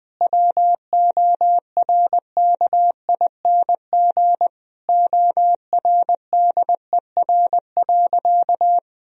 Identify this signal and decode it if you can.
Morse code, heard as a keyed tone, about 20 wpm, 700 Hz. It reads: WORKING ORDER.
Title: Sentences from Top 1000 Words - Review Part 4 - 20wpm